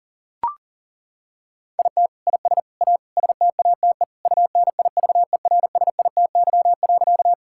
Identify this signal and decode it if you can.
Morse code, heard as a keyed tone, about 40 wpm, 700 Hz. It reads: IT IS A STATE UNIVERSITY.
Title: Sentences from Top 1000 Words - Review Part 5 - 40wpm